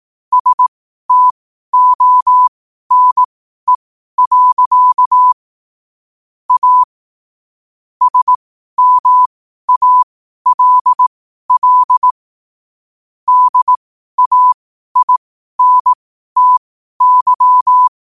Morse code, heard as a keyed tone, about 18 wpm, spaced as for 12 wpm, 1000 Hz. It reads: STONE. A SMALL DAINTY
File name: qotdfiles / 12wpm